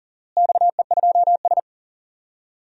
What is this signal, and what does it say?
Morse code, heard as a keyed tone, about 37 wpm, 700 Hz. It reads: XE2S
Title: DXCC-100 Call Signs - 4 Characters - Repeated 3x - 40wpm